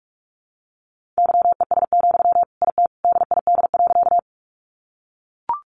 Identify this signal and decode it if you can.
Morse code, heard as a keyed tone, about 44 wpm, 700 Hz. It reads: YES, IT DID.
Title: Sentences from Top 500 Words 45wpm